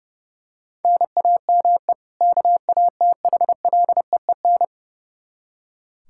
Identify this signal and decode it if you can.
Morse code, heard as a keyed tone, about 32 wpm, 700 Hz. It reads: NAME KATHLEEN
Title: Practice QSO - Exchange 1 - Beginning - 30wpm